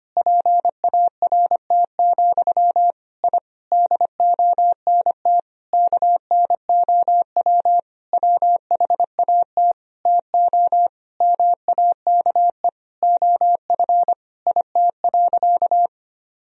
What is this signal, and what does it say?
Morse code, heard as a keyed tone, about 25 wpm, 700 Hz. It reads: PART, I DONT KNOW WHAT TO MAKE OF IT.
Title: Sentences from Top 1000 Words - Review Part 1 - 25wpm